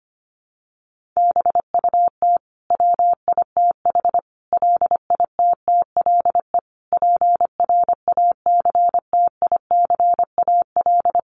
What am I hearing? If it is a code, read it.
Morse code, heard as a keyed tone, about 25 wpm, 700 Hz. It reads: BUT WITH LITTLE PRACTICAL
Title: Sentences from Top 1400 Words 25wpm